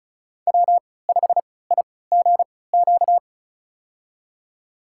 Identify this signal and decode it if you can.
Morse code, heard as a keyed tone, about 36 wpm, spaced as for 20 wpm, 700 Hz. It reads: W5IGQ
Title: USA Call Signs - 3x Character Spacing - 35wpm